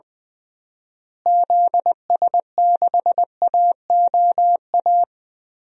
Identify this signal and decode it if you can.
Morse code, heard as a keyed tone, about 20 wpm, 700 Hz. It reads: ZS6AOA
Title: DXCC-100 Call Signs - 6 Characters - Repeated 3x - 20wpm